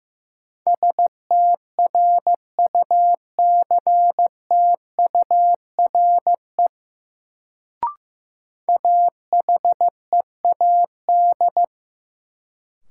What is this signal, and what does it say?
Morse code, heard as a keyed tone, about 15 wpm, 700 Hz. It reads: STRUCTURE AHEAD
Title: Sentences from Top 1000 Words - Review Part 6 - New Word Drill - 15wpm